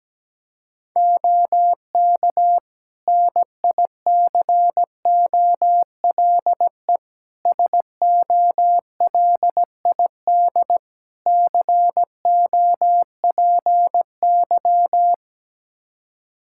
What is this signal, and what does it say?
Morse code, heard as a keyed tone, about 17 wpm, 700 Hz. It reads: OK NICOLE SOLID COPY
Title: Practice QSO - Exchange 3 - More Info - 17wpm